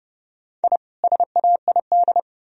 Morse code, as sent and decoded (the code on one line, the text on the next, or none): .. / ... .- .. -..
I SAID